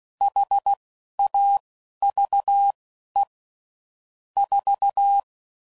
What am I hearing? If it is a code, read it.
Morse code, heard as a keyed tone, about 16 wpm, spaced as for 11 wpm, 800 Hz. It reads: HAVE 4
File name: CW Code Groups @10WPM/Technical Subjects